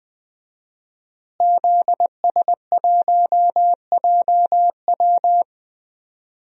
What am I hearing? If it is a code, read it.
Morse code, heard as a keyed tone, about 20 wpm, 700 Hz. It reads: ZS1JW
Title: DXCC-100 Call Signs - 5 Characters - Repeated 3x - 20wpm